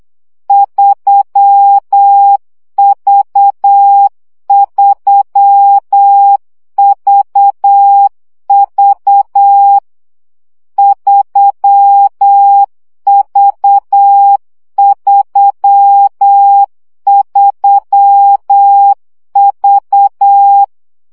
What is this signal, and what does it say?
Morse code, heard as a keyed tone, about 8 wpm, 800 Hz. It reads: 3V3VV 3V33V
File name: CW gyakorlas v 3 7 baud